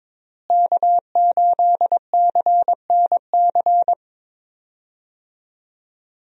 Morse code, heard as a keyed tone, about 22 wpm, 700 Hz. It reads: K8CNC
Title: USA Call Signs - 5 Characters - Repeated 3x - 22wpm